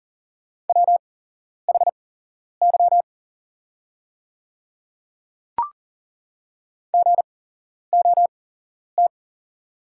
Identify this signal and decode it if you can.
Morse code, heard as a keyed tone, about 40 wpm, spaced as for 11 wpm, 690 Hz. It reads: WHY GOT
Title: 8x Character Spacing - Top 500 Words - 40wpm